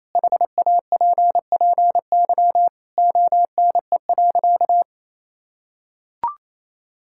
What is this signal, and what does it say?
Morse code, heard as a keyed tone, about 28 wpm, 700 Hz. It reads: HAPPY ONE.
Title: Sentences from Top 1000 Words - Review Part 5 - 28wpm